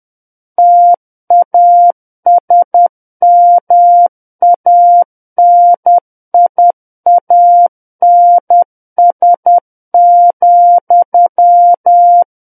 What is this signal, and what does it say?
Morse code, heard as a keyed tone, about 10 wpm, 700 Hz. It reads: TASMANIANS,
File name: Book One / 01-Chapter 1 - 10 WPM